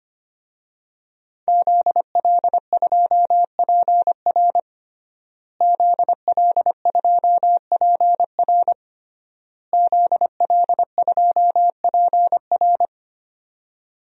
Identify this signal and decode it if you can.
Morse code, heard as a keyed tone, about 25 wpm, 700 Hz. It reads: ZL2PR ZL2PR ZL2PR
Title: DXCC-100 Call Signs - 5 Characters - Repeated 3x - 25wpm